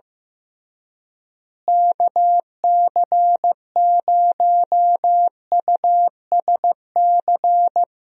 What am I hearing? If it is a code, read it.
Morse code, heard as a keyed tone, about 15 wpm, 700 Hz. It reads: KC0USC
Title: USA Call Signs - 6 Characters - Repeated 3x - 15wpm